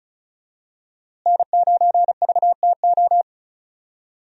Morse code, heard as a keyed tone, about 36 wpm, 700 Hz. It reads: N9VTO
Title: USA Call Signs - 5 Characters - Repeated 3x - 35wpm